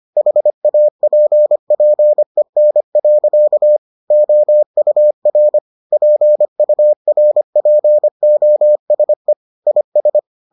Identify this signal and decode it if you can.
Morse code, heard as a keyed tone, about 25 wpm, 600 Hz. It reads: HAPPEN. OUR PURPOSE IS